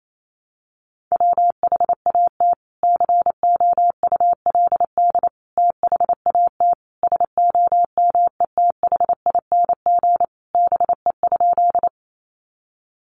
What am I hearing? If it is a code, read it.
Morse code, heard as a keyed tone, about 28 wpm, 700 Hz. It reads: WHAT COULD THAT SOMETHING BE?
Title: Sentences from Top 600 Words 28wpm